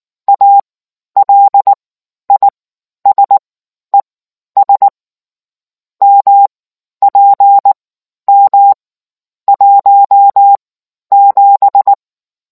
Morse code, heard as a keyed tone, about 19 wpm, spaced as for 11 wpm, 800 Hz. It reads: ALISES MPM 17